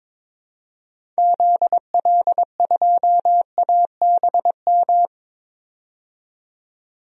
Morse code, heard as a keyed tone, about 22 wpm, 700 Hz. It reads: ZL2ABM
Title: DXCC-100 Call Signs - 6 Characters - 22wpm